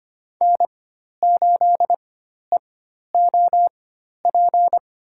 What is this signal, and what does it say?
Morse code, heard as a keyed tone, about 25 wpm, spaced as for 12 wpm, 700 Hz. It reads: N8EOP
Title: USA Call Signs - 5 Characters - 4x Character Spacing - 25wpm